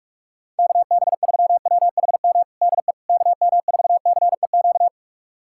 Morse code, heard as a keyed tone, about 46 wpm, 700 Hz. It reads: KB3WHM DE KM4CEQ
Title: USA Callsign DE Callsign - 6 Characters - 45wpm